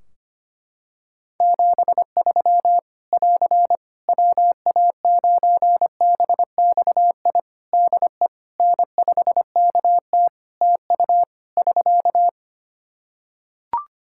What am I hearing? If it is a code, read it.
Morse code, heard as a keyed tone, about 25 wpm, 700 Hz. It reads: 73 <AR> WA9BXI DE N5KT TU <SK>
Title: Practice QSO - Exchange 4 - Ending - 25wpm